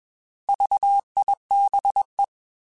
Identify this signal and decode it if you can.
Morse code, heard as a keyed tone, about 21 wpm, 800 Hz. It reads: VIBE
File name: Alphabetical / Alphabetical mp3s / Vs @20